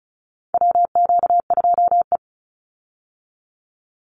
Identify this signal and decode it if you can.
Morse code, heard as a keyed tone, about 34 wpm, 700 Hz. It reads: WQ2E
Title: Call Signs Repeated 3x - 35wpm